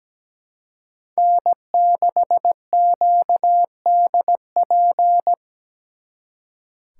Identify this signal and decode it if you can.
Morse code, heard as a keyed tone, about 20 wpm, 700 Hz. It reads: N6QDP
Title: Speed Racing - USA 1x3 Call Signs 17wpm